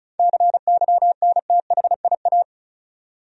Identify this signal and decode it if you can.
Morse code, heard as a keyed tone, about 35 wpm, 700 Hz. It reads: CYNTHIA
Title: USA Common Names - 35wpm